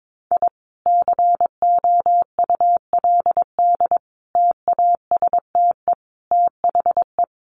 Morse code, heard as a keyed tone, about 22 wpm, 700 Hz. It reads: I COULD TASTE THE
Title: Sentences from Top 1200 Words 22wpm